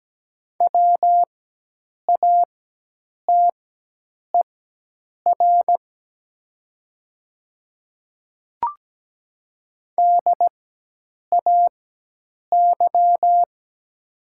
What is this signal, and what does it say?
Morse code, heard as a keyed tone, about 17 wpm, spaced as for 8 wpm, 700 Hz. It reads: WATER DAY
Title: 4x Character Spacing - Top 100 Words - 17wpm